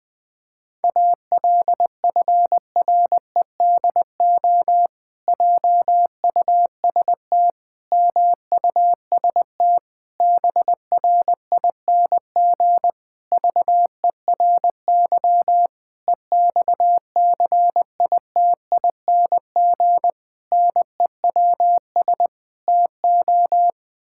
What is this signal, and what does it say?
Morse code, heard as a keyed tone, about 20 wpm, 700 Hz. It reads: ALFREDO JUST MUST BRING VERY EXCITING NEWS TO
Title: Pangram Challenge v2 - 20wpm